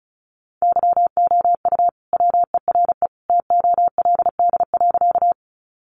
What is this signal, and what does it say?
Morse code, heard as a keyed tone, about 34 wpm, 700 Hz. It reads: YOU WERE TOLD.
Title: Sentences from Top 400 Words 35wpm